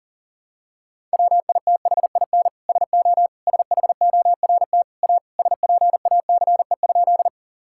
Morse code, heard as a keyed tone, about 37 wpm, 700 Hz. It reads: WITHIN SO SHORT A SPACE?
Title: Sentences from Top 1000 Words - Review Part 5 - 40wpm